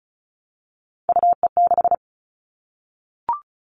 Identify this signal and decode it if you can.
Morse code, heard as a keyed tone, about 34 wpm, 700 Hz. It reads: UE6
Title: All Cut Numbers - 3-Digit 35wpm